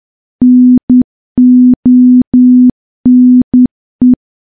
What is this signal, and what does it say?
Morse code, heard as a keyed tone, about 10 wpm, 250 Hz. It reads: NONE
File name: morse / morse-training-10WPM